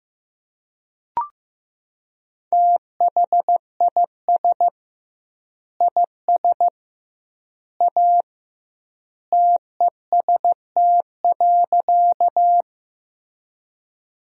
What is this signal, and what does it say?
Morse code, heard as a keyed tone, about 15 wpm, 700 Hz. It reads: THIS IS A TEST.
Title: 2x Word-Spacing - Sentences from Top 500 Words - 15wpm